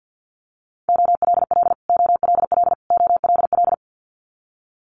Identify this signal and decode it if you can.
Morse code, heard as a keyed tone, about 50 wpm, 700 Hz. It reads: OLL OLL OLL
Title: Multi-letter Phonemes Repeated 3x 50wpm